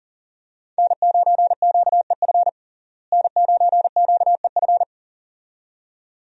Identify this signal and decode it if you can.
Morse code, heard as a keyed tone, about 40 wpm, 700 Hz. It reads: N9QEF N9QEF
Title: USA Call Signs - 5 Characters - Repeated 2x - 40wpm